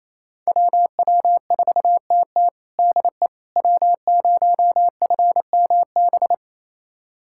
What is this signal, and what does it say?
Morse code, heard as a keyed tone, about 28 wpm, 700 Hz. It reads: WW4TT DE W0FMB
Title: USA Callsign DE Callsign - 5 Characters - 28wpm